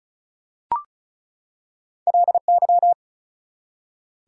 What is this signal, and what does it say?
Morse code, heard as a keyed tone, about 34 wpm, 700 Hz. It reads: LY
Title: Prefixes and Suffixes 35wpm